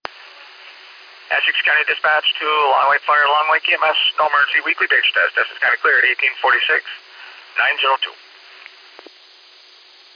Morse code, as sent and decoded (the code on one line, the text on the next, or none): none